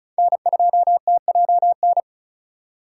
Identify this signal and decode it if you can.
Morse code, heard as a keyed tone, about 36 wpm, 700 Hz. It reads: N2TJN
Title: USA Call Signs - 5 Characters - 8x Character Spacing - 35wpm